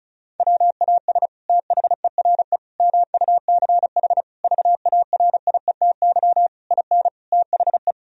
Morse code, heard as a keyed tone, about 34 wpm, 700 Hz. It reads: WAS THERE MUCH VARIETY IN THE
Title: Sentences from Top 1000 Words - Review Part 4 - 35wpm